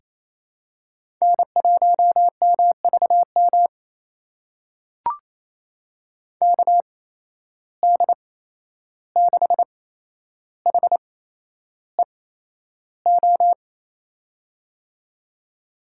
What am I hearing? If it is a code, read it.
Morse code, heard as a keyed tone, about 28 wpm, spaced as for 11 wpm, 700 Hz. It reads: N1MVM KD6HEO
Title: USA Call Signs - 8x Character Spacing - 28wpm